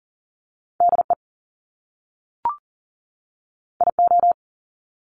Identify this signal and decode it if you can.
Morse code, heard as a keyed tone, about 39 wpm, 700 Hz. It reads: DE IO
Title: Top Two Letter Combinations in Words 40wpm